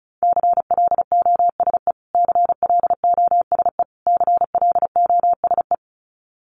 Morse code, heard as a keyed tone, about 34 wpm, 700 Hz. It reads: CLOSE CLOSE CLOSE
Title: Top 500 Words Repeated 35wpm